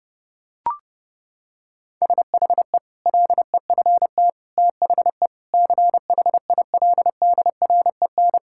Morse code, heard as a keyed tone, about 32 wpm, 700 Hz. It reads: SHE LEFT THE CHILDREN
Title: Sentences from Top 1000 Words - Review Part 5 - 30wpm